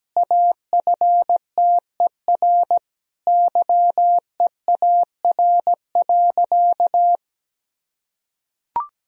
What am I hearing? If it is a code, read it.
Morse code, heard as a keyed tone, about 17 wpm, 700 Hz. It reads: AFTER YEAR.
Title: Sentences from Top 1000 Words - Review Part 1 - 17wpm